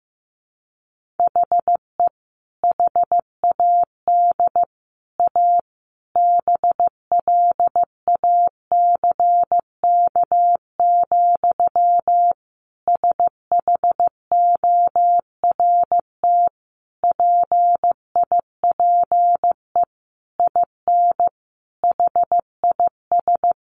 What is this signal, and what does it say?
Morse code, heard as a keyed tone, about 15 wpm, 700 Hz. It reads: HE HAD A BLACK, SHORT PIPE IN HIS
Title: Sentences from Top 1500 Words 15wpm